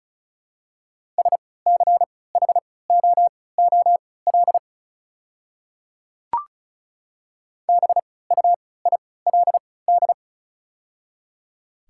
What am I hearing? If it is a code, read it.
Morse code, heard as a keyed tone, about 34 wpm, spaced as for 20 wpm, 700 Hz. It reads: SCHOOL BUILD
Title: 3x Character Spacing - Top 300 Words - 35wpm